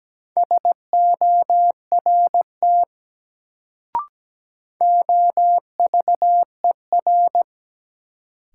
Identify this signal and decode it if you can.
Morse code, heard as a keyed tone, about 17 wpm, 700 Hz. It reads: SORT OVER
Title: Sets of 2 Words - New Word Drill - 17wpm